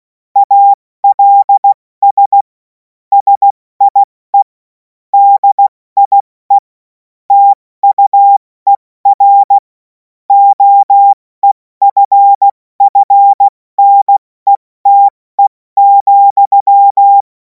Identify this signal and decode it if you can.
Morse code, heard as a keyed tone, about 16 wpm, 800 Hz. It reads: ALS SIE DIE TUER OEFFNETE,
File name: Morsekurs / Witze / Witz 07. April 2026 70 F80